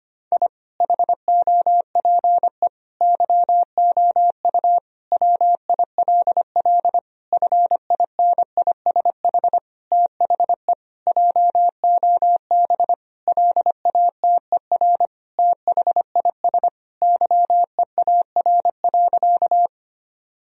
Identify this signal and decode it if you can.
Morse code, heard as a keyed tone, about 25 wpm, 700 Hz. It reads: I HOPE YOU WILL FINISH THE JOB LATER THIS YEAR.
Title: Sentences from Top 1000 Words - Review Part 3 - 25wpm